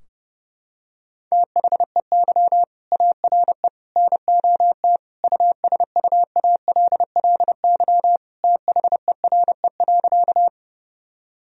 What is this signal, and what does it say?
Morse code, heard as a keyed tone, about 30 wpm, 700 Hz. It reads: THEY ARE NOT USUALLY THERE.
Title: Sentences from Top 1000 Words - Review Part 3 - 30wpm